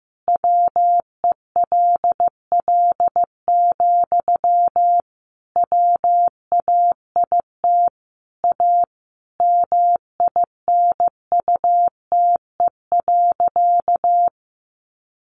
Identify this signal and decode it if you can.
Morse code, heard as a keyed tone, about 15 wpm, 700 Hz. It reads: WELL, WAIT A MINUTE.
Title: Sentences from Top 500 Words 15wpm